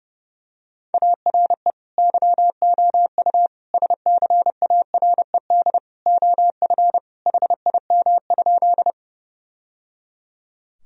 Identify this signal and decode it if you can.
Morse code, heard as a keyed tone, about 30 wpm, 700 Hz. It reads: ARE YOU SCARED OF HIM?
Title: Sentences from Top 1000 Words - Review Part 4 - 30wpm